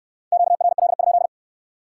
Morse code, heard as a keyed tone, about 61 wpm, 700 Hz. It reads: QRL?